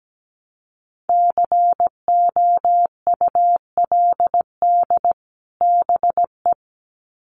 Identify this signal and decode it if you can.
Morse code, heard as a keyed tone, about 17 wpm, 700 Hz. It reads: COULD BE
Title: Sets of 2 Words 17wpm